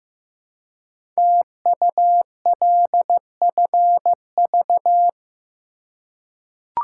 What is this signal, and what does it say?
Morse code, heard as a keyed tone, about 15 wpm, 700 Hz. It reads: TULFV
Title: Code Groups - 5 Letters - 15wpm